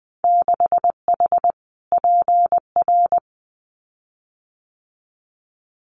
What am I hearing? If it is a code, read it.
Morse code, heard as a keyed tone, about 20 wpm, 700 Hz. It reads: 6H PR
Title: Winter Field Day Exchanges Encore 20wpm